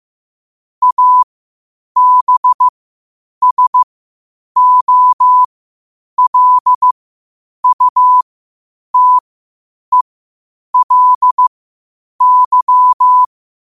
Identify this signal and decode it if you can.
Morse code, heard as a keyed tone, about 15 wpm, spaced as for 8 wpm, 1000 Hz. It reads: ABSOLUTELY
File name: qotdfiles / 8wpm